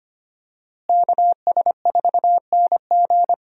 Morse code, heard as a keyed tone, about 25 wpm, 700 Hz. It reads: KS4NG
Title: USA Call Signs - 5 Characters - Repeated 2x - 25wpm